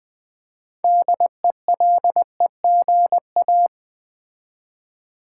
Morse code, heard as a keyed tone, about 20 wpm, 700 Hz. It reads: DELEGA
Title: Build A Word - 3 Letters At A Time - 8-Letter Words - 20wpm